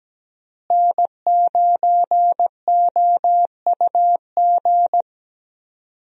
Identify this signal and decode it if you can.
Morse code, heard as a keyed tone, about 17 wpm, 700 Hz. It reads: N9OUG
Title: USA Call Signs - 5 Characters - Repeated 2x - 17wpm